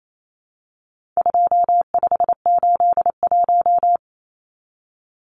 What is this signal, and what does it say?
Morse code, heard as a keyed tone, about 28 wpm, 700 Hz. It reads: 2581
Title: Numbers - 4-Digit 28wpm